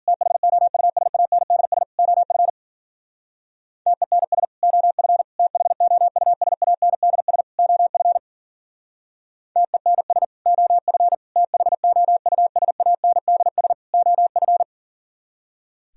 Morse code, heard as a keyed tone, about 50 wpm, 700 Hz. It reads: THOUSANDS OF TENS OF THOUSANDS OF TENS OF THOUSANDS OF